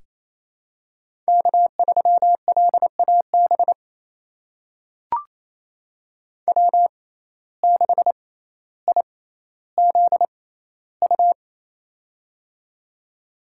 Morse code, heard as a keyed tone, about 28 wpm, spaced as for 14 wpm, 700 Hz. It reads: K3LAB W6IZU
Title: USA Call Signs - 5 Characters - 6x Character Spacing - 28wpm